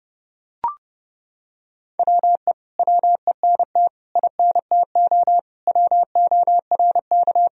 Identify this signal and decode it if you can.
Morse code, heard as a keyed tone, about 30 wpm, 700 Hz. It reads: WE WENT INTO WORK
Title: Sentences from Top 1000 Words - Review Part 4 - 30wpm